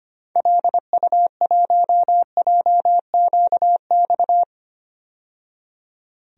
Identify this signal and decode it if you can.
Morse code, heard as a keyed tone, about 25 wpm, 700 Hz. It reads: LU1JQX
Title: DXCC-100 Call Signs - 6 Characters - 25wpm